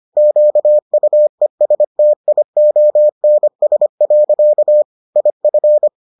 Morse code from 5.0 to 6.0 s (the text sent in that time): IF